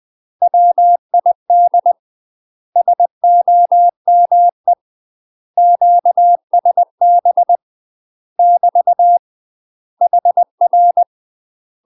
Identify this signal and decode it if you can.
Morse code, heard as a keyed tone, about 20 wpm, 700 Hz. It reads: WID SOME QSB <BT> HR
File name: UK MASTER QSOs TRAINING / 20WPM / UK MASTER QSOs 0063